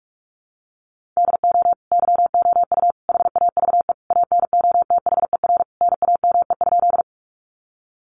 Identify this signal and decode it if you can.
Morse code, heard as a keyed tone, about 44 wpm, 700 Hz. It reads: DO YOU HAVE ANOTHER NAME?
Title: Sentences from Top 600 Words 45wpm